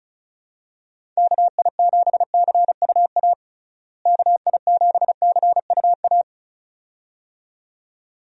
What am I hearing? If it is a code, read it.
Morse code, heard as a keyed tone, about 36 wpm, 700 Hz. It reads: KI7CUA KI7CUA
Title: USA Call Signs - 6 Characters - Repeated 2x - 35wpm